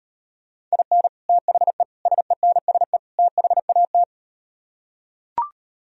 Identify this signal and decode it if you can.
Morse code, heard as a keyed tone, about 38 wpm, 700 Hz. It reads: IN THE SENSE THAT